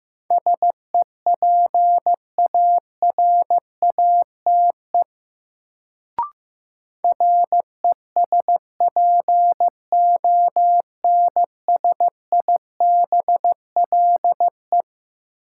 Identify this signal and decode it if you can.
Morse code, heard as a keyed tone, about 15 wpm, 700 Hz. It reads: SEPARATE RESPONSIBLE
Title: Sentences from Top 1000 Words - Review Part 7 - New Word Drill - 15wpm